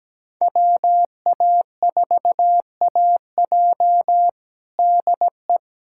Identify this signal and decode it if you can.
Morse code, heard as a keyed tone, about 17 wpm, 700 Hz. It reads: WA4AJ DE
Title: USA Callsign DE Callsign 17wpm